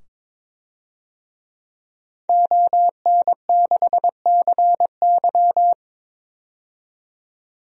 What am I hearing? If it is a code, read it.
Morse code, heard as a keyed tone, about 21 wpm, 700 Hz. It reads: ON6CY
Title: DXCC-100 Call Signs - 5 Characters - 22wpm